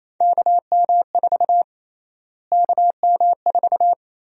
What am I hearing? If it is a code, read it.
Morse code, heard as a keyed tone, about 28 wpm, 700 Hz. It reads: KM4 KM4
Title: USA Call Sign Prefixes - Repeated 2x - 28wpm